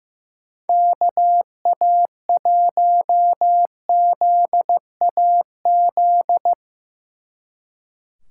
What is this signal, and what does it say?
Morse code, heard as a keyed tone, about 16 wpm, 700 Hz. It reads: KA1ZAZ
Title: Speed Racing - USA Call Signs 15wpm